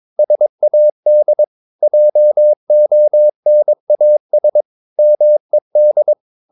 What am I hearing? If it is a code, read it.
Morse code, heard as a keyed tone, about 22 wpm, 600 Hz. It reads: SAD JONAS MED
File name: Ferie QRS 22wpm